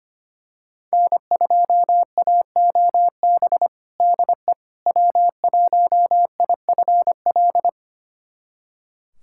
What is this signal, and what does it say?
Morse code, heard as a keyed tone, about 25 wpm, 700 Hz. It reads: N2AOB DE W1IFL
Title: USA Callsign DE Callsign - 5 Characters - 25wpm